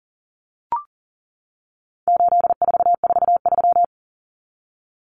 Morse code, heard as a keyed tone, about 40 wpm, 700 Hz. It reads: 8443
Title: Common Cut Numbers - 4-Digit 40wpm